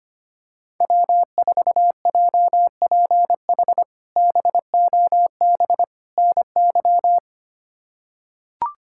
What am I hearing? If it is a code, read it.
Morse code, heard as a keyed tone, about 25 wpm, 700 Hz. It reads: W4JPH BOB NY
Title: K1USN SST QSO Practice - 25wpm